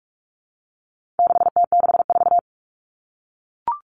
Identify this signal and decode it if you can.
Morse code, heard as a keyed tone, about 44 wpm, 700 Hz. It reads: 6T64
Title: All Cut Numbers - 4-Digit 45wpm